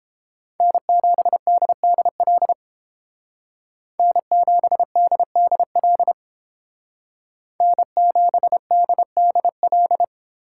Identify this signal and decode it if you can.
Morse code, heard as a keyed tone, about 31 wpm, 700 Hz. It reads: N7DDL N7DDL N7DDL